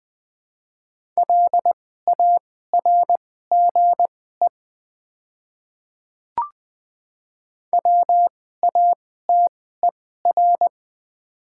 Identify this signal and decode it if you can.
Morse code, heard as a keyed tone, about 20 wpm, spaced as for 14 wpm, 700 Hz. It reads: LARGE WATER
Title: 2x Character Spacing - Top 200 Words - 20wpm